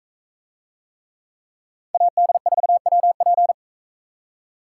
Morse code, heard as a keyed tone, about 40 wpm, 700 Hz. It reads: AD4WP